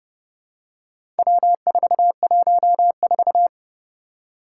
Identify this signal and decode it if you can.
Morse code, heard as a keyed tone, about 30 wpm, 700 Hz. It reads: W414
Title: Four Challenging Letter-Number Combinations 30wpm